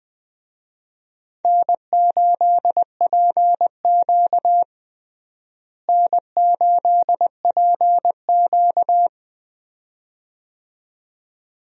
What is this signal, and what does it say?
Morse code, heard as a keyed tone, about 20 wpm, 700 Hz. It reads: N8PQ N8PQ
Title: USA Call Signs - 4 Characters - Repeated 2x - 20wpm